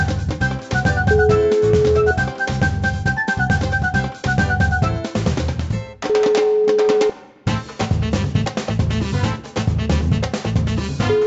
Ringback tone (Europe, 425 Hz), with noise.